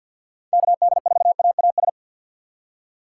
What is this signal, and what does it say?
Morse code, heard as a keyed tone, about 47 wpm, 700 Hz. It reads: KD4AAS
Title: USA Call Signs - 6 Characters - 6x Character Spacing - 50wpm